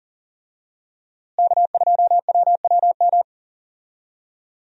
Morse code, heard as a keyed tone, about 40 wpm, 700 Hz. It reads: K2WWM
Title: USA Call Signs - 5 Characters - Repeated 2x - 40wpm